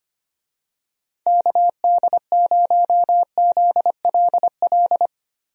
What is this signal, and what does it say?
Morse code, heard as a keyed tone, about 25 wpm, 700 Hz. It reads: KD0ZLL
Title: USA Call Signs - 6 Characters - Repeated 2x - 25wpm